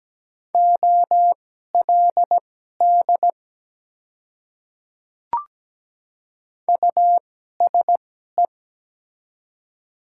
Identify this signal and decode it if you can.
Morse code, heard as a keyed tone, about 17 wpm, spaced as for 12 wpm, 700 Hz. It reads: OLD USE
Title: 2x Character Spacing - Top 200 Words - 17wpm